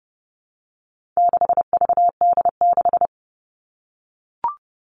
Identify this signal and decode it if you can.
Morse code, heard as a keyed tone, about 30 wpm, 700 Hz. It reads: 6VD6
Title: All Cut Numbers - 4-Digit 30wpm